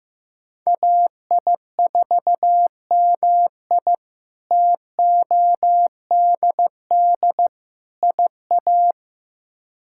Morse code, heard as a keyed tone, about 16 wpm, 700 Hz. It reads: AI4MI TODD IA
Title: K1USN SST QSO Practice - 15wpm